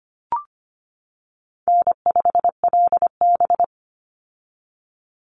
Morse code, heard as a keyed tone, about 25 wpm, 700 Hz. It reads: N5LB
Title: Easy Call Signs - 25wpm